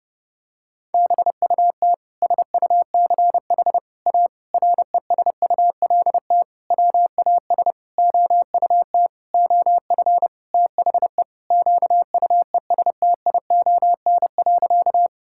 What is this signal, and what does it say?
Morse code, heard as a keyed tone, about 32 wpm, 700 Hz. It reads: BUT SUCH A RESULT WAS OUT OF THE QUESTION.
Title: Sentences from Top 1000 Words - Review Part 5 - 30wpm